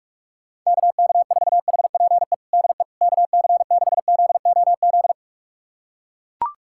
Morse code, heard as a keyed tone, about 42 wpm, 700 Hz. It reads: KK4HPE DE KC6ZOZ
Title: USA Callsign DE Callsign - 6 Characters - 45wpm